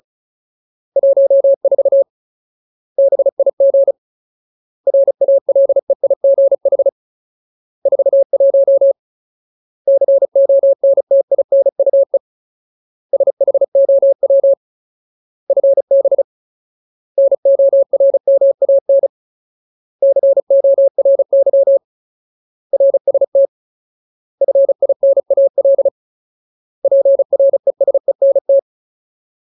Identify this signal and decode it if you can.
Morse code, heard as a keyed tone, about 34 wpm, 550 Hz. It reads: 14 BIG RALEIGH 41 CONTINUE SHOW FB NORMAN CORY RST FINAL PRESENT
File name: Ultimate Word List / 35WPM Ultimate